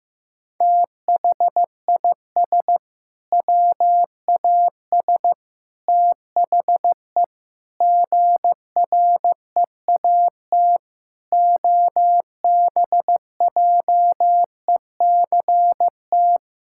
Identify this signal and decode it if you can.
Morse code, heard as a keyed tone, about 15 wpm, 700 Hz. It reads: THIS WAS THE GREAT OBJECT
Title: Sentences from Top 1000 Words - Review Part 5 - 15wpm